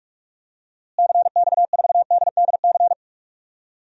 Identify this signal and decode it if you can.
Morse code, heard as a keyed tone, about 46 wpm, 700 Hz. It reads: CX4DDC
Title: DXCC-100 Call Signs - 6 Characters - Repeated 3x - 45wpm